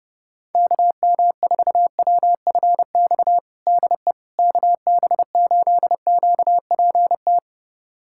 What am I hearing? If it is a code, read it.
Morse code, heard as a keyed tone, about 32 wpm, 700 Hz. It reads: KM4WFX DE KB8QPT
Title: USA Callsign DE Callsign - 6 Characters - 30wpm